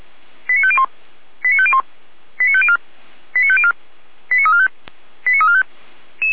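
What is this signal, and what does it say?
Five-tone selective call (ZVEI), heard as quick runs of five tones, tones of about 70 ms. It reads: 89501 89501 89504 89504 89345 89345
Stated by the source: ZVEI five-tone call received by SDR, from audio.f85360.feuerwehr-orlamuende.org (Feuerwehr Orlamünde).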